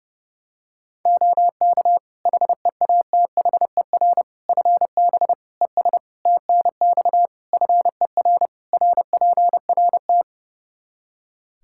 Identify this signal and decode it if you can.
Morse code, heard as a keyed tone, about 32 wpm, 700 Hz. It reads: OK HEATHER FB ES TNX FER RPRT
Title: K1USN SST QSO Practice 30wpm